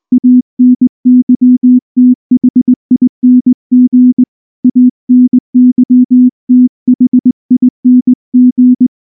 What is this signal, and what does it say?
Morse code, heard as a keyed tone, about 19 wpm, 250 Hz. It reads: ANYTHING ANYTHING
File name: morse / morse-training-20WPM-1